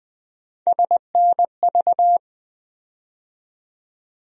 Morse code, heard as a keyed tone, about 20 wpm, 700 Hz. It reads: SNV
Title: USA Call Sign Postfixes - 3 Letters 20wpm